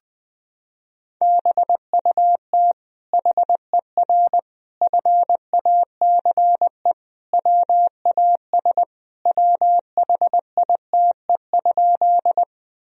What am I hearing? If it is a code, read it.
Morse code, heard as a keyed tone, about 20 wpm, 700 Hz. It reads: BUT HER FACE WAS WHITE?
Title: Sentences from Top 1000 Words - Review Part 2 - 20wpm